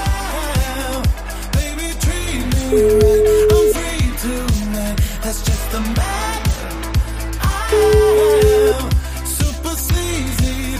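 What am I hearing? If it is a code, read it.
Ringback tone (Europe, 425 Hz), with noise.
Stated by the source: O2 Czech Republic welcome melody that callers hear before the call is answered.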